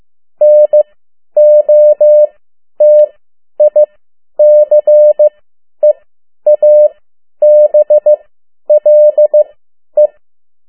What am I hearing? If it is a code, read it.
Morse code, heard as a keyed tone, about 15 wpm, spaced as for 10 wpm, 600 Hz. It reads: NOTICEABLE
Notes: Morse code 10 WPM